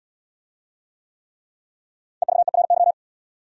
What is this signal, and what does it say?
Morse code, heard as a keyed tone, about 79 wpm, 700 Hz. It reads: EVERY